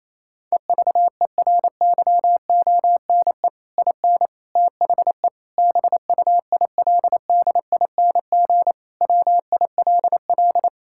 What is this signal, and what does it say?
Morse code, heard as a keyed tone, about 28 wpm, 700 Hz. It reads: EVERYONE IN THE BUILDING WILL
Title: Sentences from Top 1000 Words - Review Part 6 - 28wpm